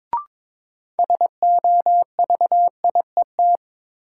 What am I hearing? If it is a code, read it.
Morse code, heard as a keyed tone, about 22 wpm, 700 Hz. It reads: SOVIET
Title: Sets of 5 Words Encore - New Word Drill - 22wpm